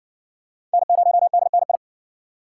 Morse code, heard as a keyed tone, about 60 wpm, 700 Hz. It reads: N0DNI